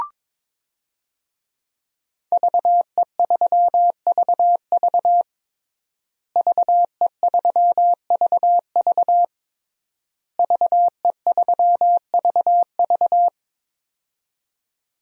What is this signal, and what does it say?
Morse code, heard as a keyed tone, about 23 wpm, 700 Hz. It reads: VE3VV VE3VV VE3VV
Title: DXCC-100 Call Signs - 5 Characters - Repeated 3x - 22wpm